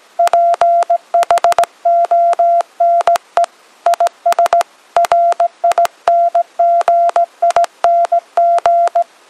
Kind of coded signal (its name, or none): Morse code